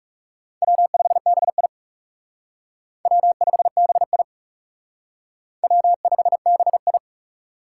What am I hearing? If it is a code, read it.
Morse code, heard as a keyed tone, about 37 wpm, 700 Hz. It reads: W5BI W5BI W5BI
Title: Speed Racing - USA Call Signs 30wpm